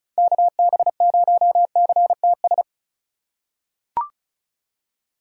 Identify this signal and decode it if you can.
Morse code, heard as a keyed tone, about 36 wpm, 700 Hz. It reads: KB0CTS
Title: USA Call Signs - 4x Character Spacing - 35wpm